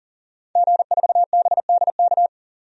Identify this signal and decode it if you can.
Morse code, heard as a keyed tone, about 40 wpm, 700 Hz. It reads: G4BDK
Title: DXCC-100 Call Signs - 5 Characters - 40wpm